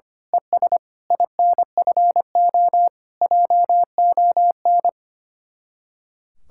Morse code, heard as a keyed tone, about 24 wpm, 700 Hz. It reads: ES INFO JON
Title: Speed-Racing Common QSO Phrases Encore - 25wpm